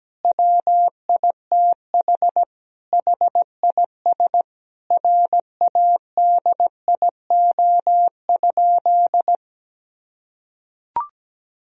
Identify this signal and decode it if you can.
Morse code, heard as a keyed tone, about 17 wpm, 700 Hz. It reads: WITH HIS RADIO?
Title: Sentences from Top 1000 Words - Review Part 3 - 17wpm